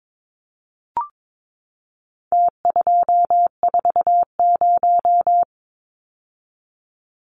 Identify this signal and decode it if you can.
Morse code, heard as a keyed tone, about 22 wpm, 700 Hz. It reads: T240
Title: Numbers - Contest Style - 4-Digit 22wpm